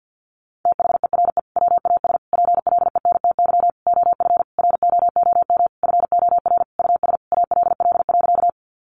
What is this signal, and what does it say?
Morse code, heard as a keyed tone, about 48 wpm, 700 Hz. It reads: THERE WAS PLENTY OF ROOM FOR US ALL.
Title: Sentences from Top 1300 Words 50wpm